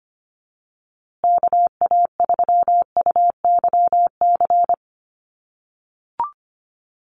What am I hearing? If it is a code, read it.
Morse code, heard as a keyed tone, about 25 wpm, 700 Hz. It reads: KA3UYC
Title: USA Call Signs Repeated 2x 25wpm